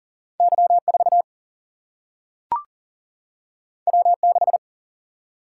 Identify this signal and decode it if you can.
Morse code, heard as a keyed tone, about 41 wpm, 700 Hz. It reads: Y4 W6
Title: Single-Letter Single-Number 40wpm